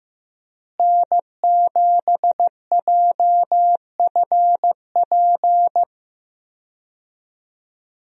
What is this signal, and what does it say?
Morse code, heard as a keyed tone, about 15 wpm, 700 Hz. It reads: N7JFP
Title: USA Call Signs - 5 Characters - Repeated 3x - 15wpm